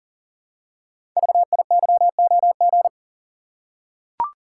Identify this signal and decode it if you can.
Morse code, heard as a keyed tone, about 38 wpm, 690 Hz. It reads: VIYOG
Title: Code Groups - 5 Letters - 40wpm